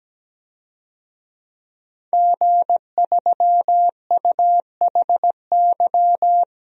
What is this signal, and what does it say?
Morse code, heard as a keyed tone, about 17 wpm, 700 Hz. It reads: G3UHY
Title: DXCC-100 Call Signs - 5 Characters - 17wpm